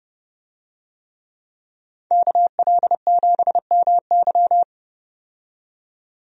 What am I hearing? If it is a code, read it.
Morse code, heard as a keyed tone, about 30 wpm, 700 Hz. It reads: KL7MY
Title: USA Call Signs - 5 Characters - 30wpm